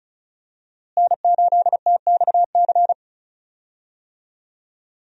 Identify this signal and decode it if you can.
Morse code, heard as a keyed tone, about 36 wpm, 700 Hz. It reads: N8TXC
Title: USA Call Signs - 5 Characters - Repeated 2x - 35wpm